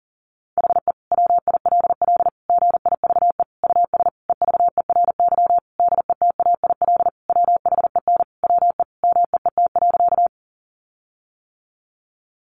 Morse code, heard as a keyed tone, about 40 wpm, 700 Hz. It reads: HE WILL GIVE US EVERY DETAIL WHEN WE MEET.
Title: Sentences from Top 1500 Words 40wpm